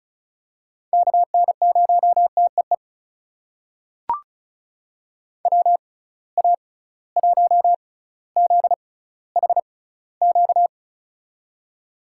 Morse code, heard as a keyed tone, about 36 wpm, spaced as for 18 wpm, 700 Hz. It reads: KN0TEE WA1ZHQ
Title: USA Call Signs - 6 Characters - 6x Character Spacing - 35wpm